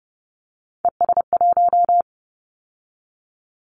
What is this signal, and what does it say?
Morse code, heard as a keyed tone, about 30 wpm, 700 Hz. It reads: ES1
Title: Callsign Prefixes for Top 50 DXCC - 30wpm